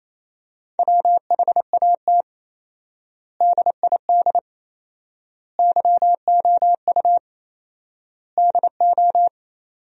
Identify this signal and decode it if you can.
Morse code, heard as a keyed tone, about 28 wpm, 700 Hz. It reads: WHAT DID YOU DO
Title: 4x Word-Spacing - Sentences from Top 100 Words - 28wpm